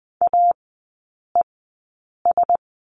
Morse code, heard as a keyed tone, about 20 wpm, spaced as for 8 wpm, 700 Hz. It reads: AES
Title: 041 Lesson 8e - Number 1 - Three Characters 20wpm